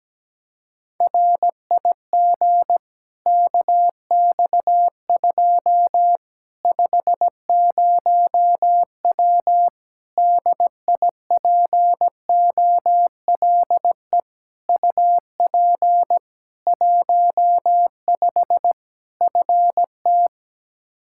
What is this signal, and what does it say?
Morse code, heard as a keyed tone, about 17 wpm, 700 Hz. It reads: RIG KX2 50W DIPOLE UP 15 FT
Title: K1USN SST QSO Practice 17wpm